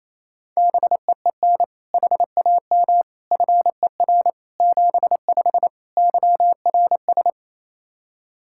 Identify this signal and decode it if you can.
Morse code, heard as a keyed tone, about 28 wpm, 700 Hz. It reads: BEEN HAM FER 75 YRS
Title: Practice QSO - Exchange 3 - More Info - 28wpm